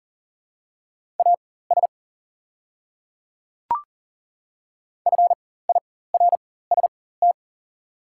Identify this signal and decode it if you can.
Morse code, heard as a keyed tone, about 40 wpm, spaced as for 19 wpm, 700 Hz. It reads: AS FIRST
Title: 4x Character Spacing - Top 100 Words - 40wpm